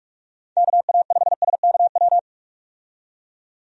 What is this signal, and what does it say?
Morse code, heard as a keyed tone, about 46 wpm, 700 Hz. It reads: KA5SKW
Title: USA Call Signs - 6 Characters - 45wpm